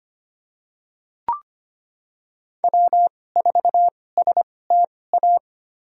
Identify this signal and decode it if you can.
Morse code, heard as a keyed tone, about 25 wpm, spaced as for 18 wpm, 700 Hz. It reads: W4STA
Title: USA Call Signs - 5 Characters - 2x Character Spacing - 25wpm